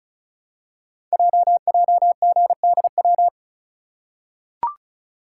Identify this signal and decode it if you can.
Morse code, heard as a keyed tone, about 36 wpm, 700 Hz. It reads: JJGDW
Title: Code Groups - 5 Letters - 35wpm